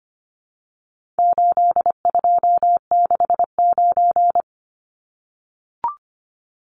Morse code, heard as a keyed tone, about 25 wpm, 700 Hz. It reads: 8269
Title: Numbers - 4-Digit 25wpm